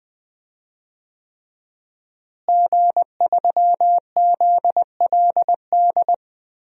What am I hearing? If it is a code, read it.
Morse code, heard as a keyed tone, about 20 wpm, 700 Hz. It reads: G3ZLD
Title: DXCC-100 Call Signs - 5 Characters - Repeated 3x - 20wpm